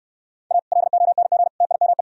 Morse code, heard as a keyed tone, about 64 wpm, 700 Hz. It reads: A LONG TIME